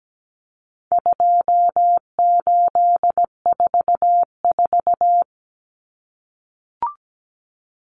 Morse code, heard as a keyed tone, about 17 wpm, 700 Hz. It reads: 2844
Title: Numbers - 4-Digit 17wpm